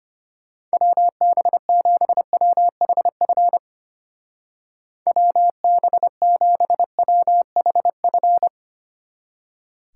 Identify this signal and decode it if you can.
Morse code, heard as a keyed tone, about 32 wpm, 700 Hz. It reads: WB7WHF WB7WHF